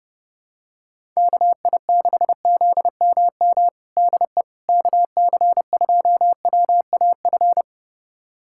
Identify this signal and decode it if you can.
Morse code, heard as a keyed tone, about 30 wpm, 700 Hz. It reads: KI6ZMM DE KC2WAF
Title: USA Callsign DE Callsign 30wpm